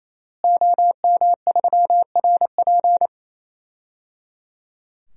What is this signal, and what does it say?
Morse code, heard as a keyed tone, about 28 wpm, 700 Hz. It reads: OM3RP
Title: DXCC-100 Call Signs - 5 Characters - Repeated 3x - 28wpm